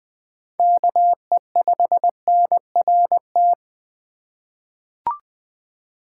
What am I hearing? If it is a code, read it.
Morse code, heard as a keyed tone, about 20 wpm, 700 Hz. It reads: KE5NRT
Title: USA Call Signs - 6 Characters - 3x Character Spacing - 20wpm